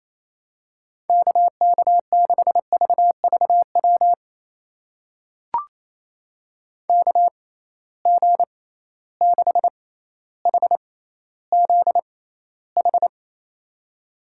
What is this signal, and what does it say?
Morse code, heard as a keyed tone, about 28 wpm, spaced as for 14 wpm, 700 Hz. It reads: KK6VVW KG6HZH
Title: USA Call Signs - 6 Characters - 6x Character Spacing - 28wpm